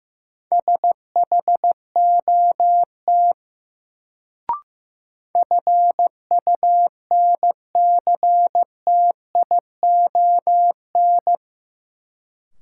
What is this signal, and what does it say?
Morse code, heard as a keyed tone, about 15 wpm, 700 Hz. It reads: SHOT FUNCTION
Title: Sentences from Top 1100 Words - New Word Drill - 15wpm